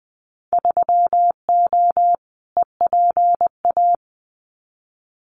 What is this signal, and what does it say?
Morse code, heard as a keyed tone, about 20 wpm, 700 Hz. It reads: 3O EPA
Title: Winter Field Day Exchanges Encore 20wpm